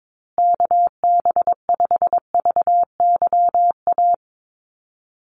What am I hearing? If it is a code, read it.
Morse code, heard as a keyed tone, about 22 wpm, 700 Hz. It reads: KB5VYA
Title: USA Call Signs Repeated 2x 22wpm